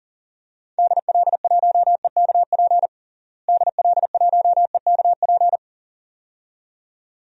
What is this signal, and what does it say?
Morse code, heard as a keyed tone, about 40 wpm, 700 Hz. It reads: DL1EKP DL1EKP
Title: DXCC-100 Call Signs - 6 Characters - Repeated 2x - 40wpm